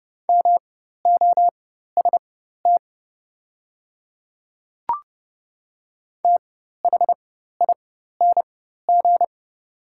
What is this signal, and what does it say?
Morse code, heard as a keyed tone, about 30 wpm, spaced as for 14 wpm, 690 Hz. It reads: MOST THING
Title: 4x Character Spacing - Top 100 Words - 30wpm